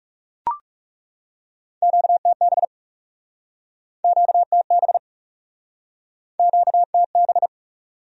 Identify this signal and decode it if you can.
Morse code, heard as a keyed tone, about 40 wpm, 700 Hz. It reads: QTB QTB QTB